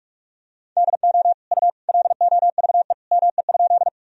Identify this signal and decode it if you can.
Morse code, heard as a keyed tone, about 46 wpm, 700 Hz. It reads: DO U LOVE ME?
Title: Sentences from Top 400 Words and Abbreviations 45wpm